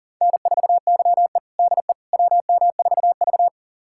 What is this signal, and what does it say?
Morse code, heard as a keyed tone, about 40 wpm, 700 Hz. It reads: N4YE DE WM4V
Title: USA Callsign DE Callsign - 4 Characters - 40wpm